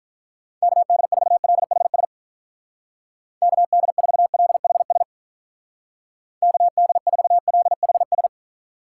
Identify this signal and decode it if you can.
Morse code, heard as a keyed tone, about 44 wpm, 700 Hz. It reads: KD4LHS KD4LHS KD4LHS